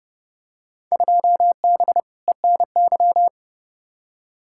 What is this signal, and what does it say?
Morse code, heard as a keyed tone, about 31 wpm, 700 Hz. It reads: 2B ENY
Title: Field Day Exchanges Encore - 30wpm